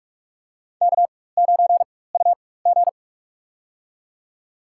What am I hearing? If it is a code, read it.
Morse code, heard as a keyed tone, about 43 wpm, spaced as for 21 wpm, 700 Hz. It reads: K9UG
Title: USA Call Signs - 4 Characters - 4x Character Spacing - 45wpm